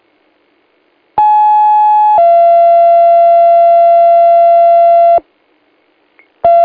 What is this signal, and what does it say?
Two-tone page, heard as a one-second tone, then a three-second tone, about 850 then 680 Hz. It is heard from 1.2 to 5.2 s.
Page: Old Forge 18:36:22 on 01/05/25 – ADK Alert